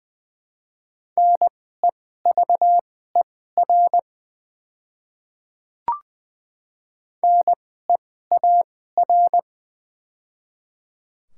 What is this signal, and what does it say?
Morse code, heard as a keyed tone, about 21 wpm, spaced as for 15 wpm, 700 Hz. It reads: NEVER NEAR
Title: 2x Character Spacing - Top 300 Words - 20wpm